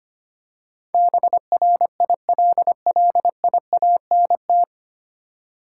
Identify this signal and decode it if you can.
Morse code, heard as a keyed tone, about 25 wpm, 700 Hz. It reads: BRILLIANT
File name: Nine Letter Words - 25wpm